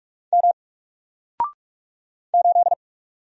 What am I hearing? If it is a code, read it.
Morse code, heard as a keyed tone, about 44 wpm, 700 Hz. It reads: M 8
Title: Single Character (Letter:Number:Punctuation:Prosign) Alt 45wpm